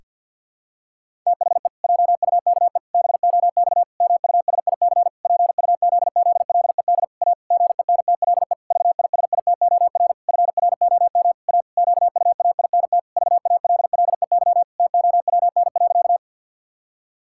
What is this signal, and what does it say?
Morse code, heard as a keyed tone, about 50 wpm, 700 Hz. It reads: THE JUKE BOX MUSIC PUZZLED A GENTLE VISITOR FROM A QUAINT VALLEY TOWN.
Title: Pangram Challenge v2 - 50wpm